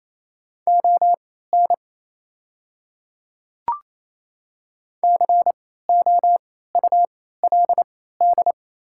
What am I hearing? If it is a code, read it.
Morse code, heard as a keyed tone, about 28 wpm, spaced as for 16 wpm, 690 Hz. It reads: ON COULD
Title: 3x Character Spacing - Top 100 Words - 28wpm